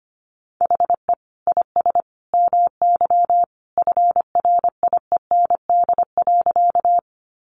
Morse code, heard as a keyed tone, about 25 wpm, 700 Hz. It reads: HE IS MY FRIEND.
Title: Sentences from Top 300 Words 25wpm